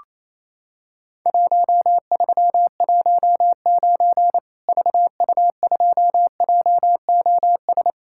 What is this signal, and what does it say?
Morse code, heard as a keyed tone, about 28 wpm, 700 Hz. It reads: 1319 VU2JOS
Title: Worked All Europe DX Contest - QTC Exchange - 28wpm